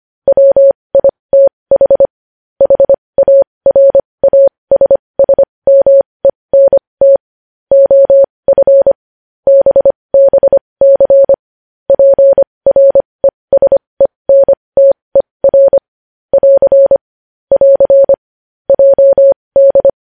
Morse code, heard as a keyed tone, about 25 wpm, spaced as for 20 wpm, 550 Hz. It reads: WITH HARASSMENT OF BBC PRESENTER <AR> <AR> JD
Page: News Headlines in Morse Code at 20 WPM